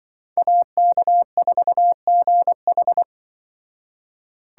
Morse code, heard as a keyed tone, about 25 wpm, 700 Hz. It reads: AK4GH